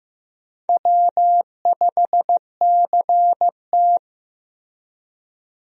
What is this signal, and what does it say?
Morse code, heard as a keyed tone, about 15 wpm, 700 Hz. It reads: W5CT
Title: USA Call Signs - 4 Characters - 15wpm